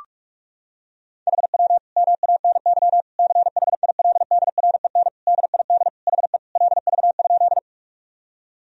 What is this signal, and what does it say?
Morse code, heard as a keyed tone, about 43 wpm, 700 Hz. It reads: HW MANY CHILDREN DID HE LV?
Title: Sentences from Top 400 Words and Abbreviations 45wpm